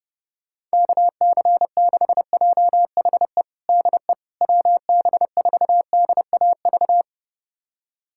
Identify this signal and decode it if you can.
Morse code, heard as a keyed tone, about 30 wpm, 700 Hz. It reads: KC6JHE DE WB4DAV
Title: USA Callsign DE Callsign 30wpm